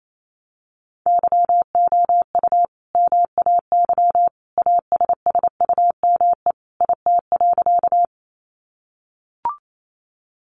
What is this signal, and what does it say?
Morse code, heard as a keyed tone, about 28 wpm, 700 Hz. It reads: YOU MAY ASSUME IT.
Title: Sentences from Top 1500 Words 28wpm